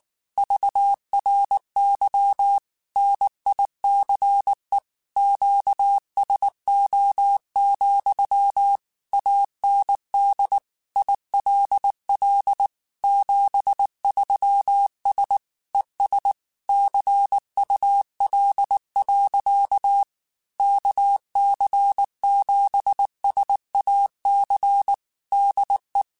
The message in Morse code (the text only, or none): VRY NICE QSO, AND ILL 73S ES CUL. KC7SAC DE